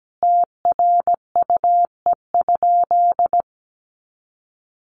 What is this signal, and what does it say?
Morse code, heard as a keyed tone, about 17 wpm, 700 Hz. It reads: TRUE?
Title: Sentences from Top 400 Words 17wpm